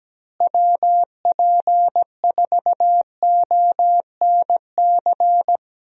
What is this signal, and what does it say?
Morse code, heard as a keyed tone, about 17 wpm, 700 Hz. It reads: WP4ONC
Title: USA Call Signs - 6 Characters - Repeated 3x - 17wpm